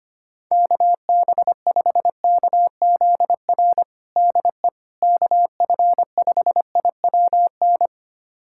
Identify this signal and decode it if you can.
Morse code, heard as a keyed tone, about 25 wpm, 700 Hz. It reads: KB5KZR DE KF5IWN
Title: USA Callsign DE Callsign - 6 Characters - 25wpm